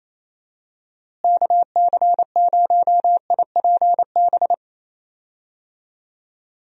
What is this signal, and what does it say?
Morse code, heard as a keyed tone, about 28 wpm, 700 Hz. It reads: KC0IPB
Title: USA Call Signs - 6 Characters - Repeated 2x - 28wpm